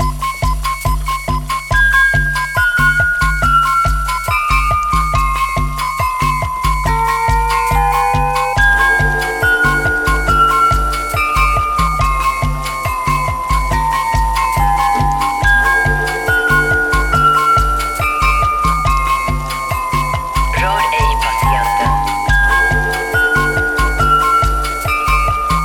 There is no code.